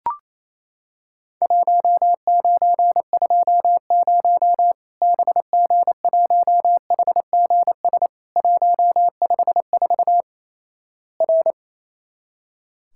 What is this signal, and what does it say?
Morse code, heard as a keyed tone, about 28 wpm, 700 Hz. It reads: 1920 BG1HGS 154 R
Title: Worked All Europe DX Contest - QTC Exchange - 28wpm